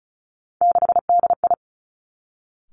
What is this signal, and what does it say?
Morse code, heard as a keyed tone, about 36 wpm, 700 Hz. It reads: 6DI
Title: Three Challenging Letter-Number Combinations 35wpm